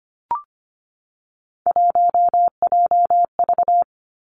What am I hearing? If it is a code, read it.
Morse code, heard as a keyed tone, about 25 wpm, 700 Hz. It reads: 1JV
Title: Three Challenging Letter-Number Combinations 25wpm